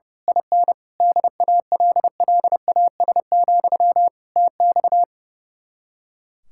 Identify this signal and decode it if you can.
Morse code, heard as a keyed tone, about 29 wpm, 700 Hz. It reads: IN DALLAS, TX
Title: Speed-Racing Common QSO Phrases - 30wpm